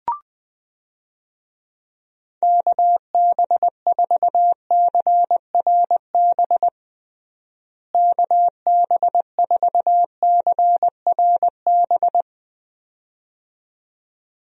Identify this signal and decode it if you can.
Morse code, heard as a keyed tone, about 20 wpm, 700 Hz. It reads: KB4CRB KB4CRB
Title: USA Call Signs - 6 Characters - Repeated 2x - 20wpm